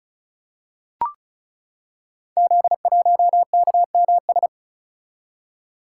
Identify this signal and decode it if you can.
Morse code, heard as a keyed tone, about 36 wpm, 700 Hz. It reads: Z1KMS
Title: Code Groups - 5 Letter:Numbers - 35wpm